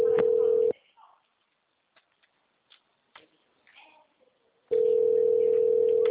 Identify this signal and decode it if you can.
Ringback tone (North America), 440 + 480 Hz, with speech.
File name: out- -2030- - - .81